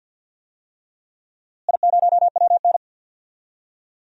Morse code, heard as a keyed tone, about 53 wpm, 700 Hz. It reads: I0WN
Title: DXCC-100 Call Signs - 4 Characters - Repeated 3x - 50wpm